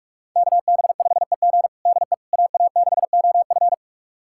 Morse code, heard as a keyed tone, about 46 wpm, 700 Hz. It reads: KB5EG DE AA6OF
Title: USA Callsign DE Callsign - 5 Characters - 45wpm